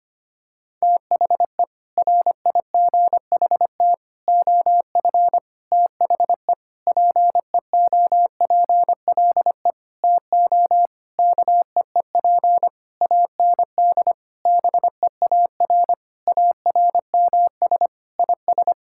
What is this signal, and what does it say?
Morse code, heard as a keyed tone, about 25 wpm, 700 Hz. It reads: THE RIGHT OF THE PEOPLE TO KEEP AND BEAR ARMS IS
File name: practice / 25wpm / Sentences with 9-Word Phrases - 25wpm